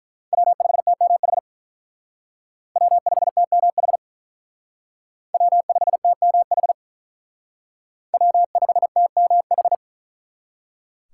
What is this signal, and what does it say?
Morse code, heard as a keyed tone, about 43 wpm, 700 Hz. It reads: W5TMH W5TMH W5TMH W5TMH
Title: Speed Racing - USA Call Signs 35wpm